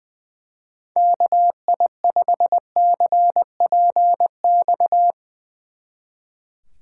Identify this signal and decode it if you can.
Morse code, heard as a keyed tone, about 22 wpm, 700 Hz. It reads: KI5CPX
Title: Speed Racing - USA Call Signs 20wpm